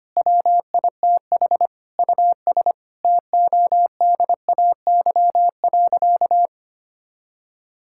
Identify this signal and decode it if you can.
Morse code, heard as a keyed tone, about 25 wpm, 700 Hz. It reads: WITH US TODAY.
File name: practice / 25wpm / Sentences with 9-Word Phrases - 25wpm